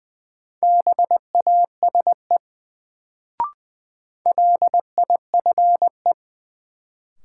Morse code, heard as a keyed tone, about 20 wpm, 700 Hz. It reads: BASE LIFE
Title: Sets of 5 Words - New Word Drill - 20wpm